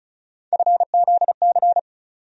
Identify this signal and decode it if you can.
Morse code, heard as a keyed tone, about 34 wpm, 700 Hz. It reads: FZC
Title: USA Call Sign Postfixes - 3 Letters 35wpm